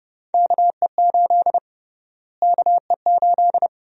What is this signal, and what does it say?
Morse code, heard as a keyed tone, about 31 wpm, 700 Hz. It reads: KE8 KE8
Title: USA Call Sign Prefixes - Repeated 2x - 30wpm